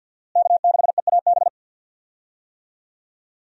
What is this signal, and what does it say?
Morse code, heard as a keyed tone, about 52 wpm, 700 Hz. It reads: K6EAB
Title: USA Call Signs - 5 Characters - Repeated 2x - 50wpm